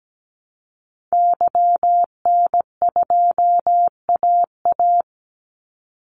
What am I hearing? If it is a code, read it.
Morse code, heard as a keyed tone, about 17 wpm, 700 Hz. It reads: YN2AA
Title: Call Signs Repeated 3x - 17wpm